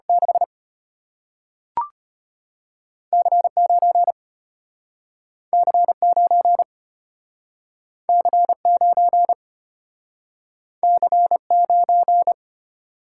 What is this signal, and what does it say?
Morse code, heard as a keyed tone, about 35 wpm, 700 Hz. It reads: 6 C9 C9 C9 C9